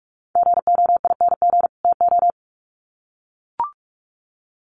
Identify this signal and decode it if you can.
Morse code, heard as a keyed tone, about 44 wpm, 700 Hz. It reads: GOING TO
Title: Sets of 2 Words 45wpm